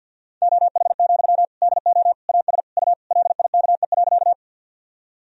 Morse code, heard as a keyed tone, about 46 wpm, 700 Hz. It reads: OH, DO AS U LIKE.
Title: Sentences from Top 500 Words and Abbreviations 50wpm